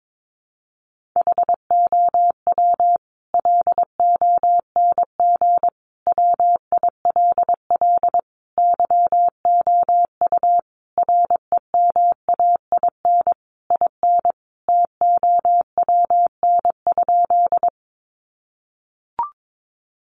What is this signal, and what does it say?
Morse code, heard as a keyed tone, about 22 wpm, 700 Hz. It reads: HOW LONG WILL YOU REMAIN IN TOWN?
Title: Sentences from Top 1200 Words 22wpm